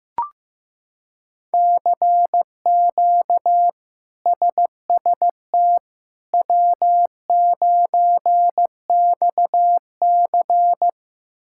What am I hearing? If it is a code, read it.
Morse code, heard as a keyed tone, about 15 wpm, 700 Hz. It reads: CQ SST W9XC
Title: K1USN SST QSO Practice - 15wpm